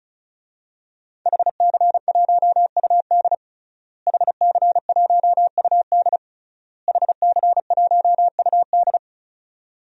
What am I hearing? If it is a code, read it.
Morse code, heard as a keyed tone, about 36 wpm, 700 Hz. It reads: HC1UD HC1UD HC1UD
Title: DXCC-100 Call Signs - 5 Characters - Repeated 3x - 35wpm